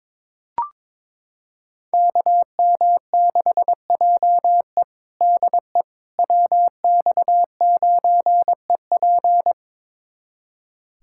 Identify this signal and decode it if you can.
Morse code, heard as a keyed tone, about 22 wpm, 700 Hz. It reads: KM6JE DE WX9EP
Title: USA Callsign DE Callsign 22wpm